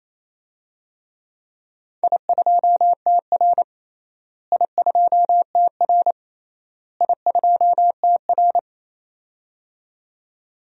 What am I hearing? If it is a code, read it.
Morse code, heard as a keyed tone, about 27 wpm, 700 Hz. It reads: I2TR I2TR I2TR
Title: DXCC-100 Call Signs - 4 Characters - Repeated 3x - 28wpm